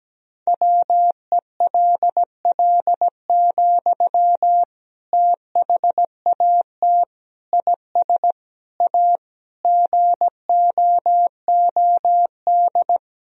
Morse code, heard as a keyed tone, about 17 wpm, 700 Hz. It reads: WELL, THAT IS A GOOD
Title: Sentences from Top 1000 Words - Review Part 2 - 17wpm